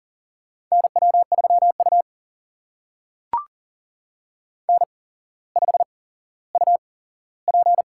Morse code, heard as a keyed tone, about 40 wpm, spaced as for 17 wpm, 700 Hz. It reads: NW3U N5UP
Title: USA Call Signs - 4 Characters - 8x Character Spacing - 40wpm